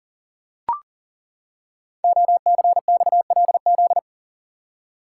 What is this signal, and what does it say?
Morse code, heard as a keyed tone, about 40 wpm, 700 Hz. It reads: OCXLZ
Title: Code Groups - 5 Letters - 40wpm